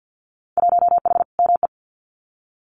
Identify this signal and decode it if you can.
Morse code, heard as a keyed tone, about 48 wpm, 700 Hz. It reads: 1H ME
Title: Winter Field Day Exchanges 50wpm